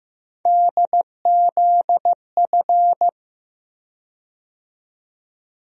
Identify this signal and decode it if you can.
Morse code, heard as a keyed tone, about 15 wpm, 700 Hz. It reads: DZF
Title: USA Call Sign Postfixes - 3 Letters 15wpm